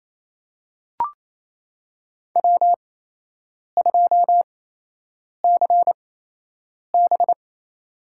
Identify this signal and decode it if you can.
Morse code, heard as a keyed tone, about 28 wpm, spaced as for 8 wpm, 700 Hz. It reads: W2CB
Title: USA Call Signs - 4 Characters - 8x Character Spacing - 28wpm